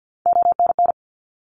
Morse code, heard as a keyed tone, about 50 wpm, 700 Hz. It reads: ONN
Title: ARRL Sections 50wpm